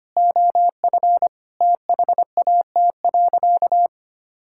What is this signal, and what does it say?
Morse code, heard as a keyed tone, about 25 wpm, 700 Hz. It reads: OF THAT.
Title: Sentences from Top 1000 Words - Review Part 6 - 25wpm